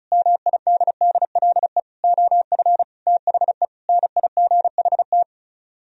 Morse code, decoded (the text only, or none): MIDDLE OF THE NIGHT